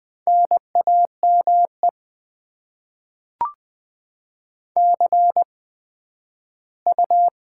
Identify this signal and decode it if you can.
Morse code, heard as a keyed tone, about 20 wpm, spaced as for 12 wpm, 700 Hz. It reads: NAME CU
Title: Top 20 QSO Elements - 8x Character Spacing - 20wpm